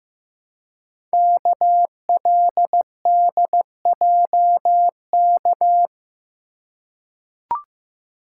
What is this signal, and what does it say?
Morse code, heard as a keyed tone, about 15 wpm, 700 Hz. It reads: KLDJK
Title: Code Groups - 5 Letters - 15wpm